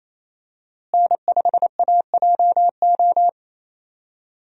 Morse code, heard as a keyed tone, about 28 wpm, 700 Hz. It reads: N5AJO
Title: USA Call Signs - 5 Characters - Repeated 3x - 28wpm